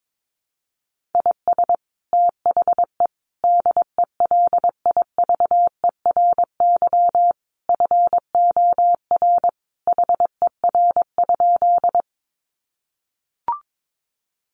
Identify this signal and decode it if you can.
Morse code, heard as a keyed tone, about 22 wpm, 700 Hz. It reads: IS THE DELIVERY FOR HER?
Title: Sentences from Top 1500 Words 22wpm